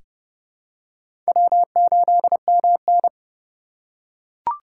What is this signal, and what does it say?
Morse code, heard as a keyed tone, about 32 wpm, 700 Hz. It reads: W8MN
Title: USA Call Signs - 4 Characters - 3x Character Spacing - 30wpm